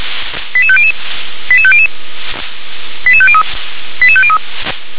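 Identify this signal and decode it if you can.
Five-tone selective call (ZVEI), heard as quick runs of five tones, tones of about 70 ms. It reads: 88500 88500 88503 88503
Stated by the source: ZVEI five-tone call 88730 received by SDR.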